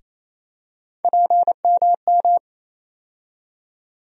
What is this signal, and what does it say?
Morse code, heard as a keyed tone, about 28 wpm, 700 Hz. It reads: PMM
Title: USA Call Sign Postfixes - 3 Letters 28wpm